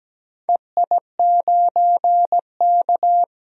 Morse code, heard as a keyed tone, about 17 wpm, 700 Hz. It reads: EI9K
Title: DXCC-100 Call Signs - 4 Characters - 17wpm